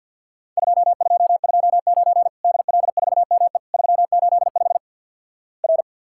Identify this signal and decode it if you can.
Morse code, heard as a keyed tone, about 49 wpm, 700 Hz. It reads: 2229 DL4ME 385 R
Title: Worked All Europe DX Contest - QTC Exchange - 50wpm